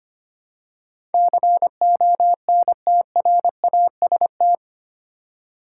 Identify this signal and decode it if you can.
Morse code, heard as a keyed tone, about 25 wpm, 700 Hz. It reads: CONTRAST
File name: practice / 25wpm / 8 Letter Words - First 3 Letters Given - 25wpm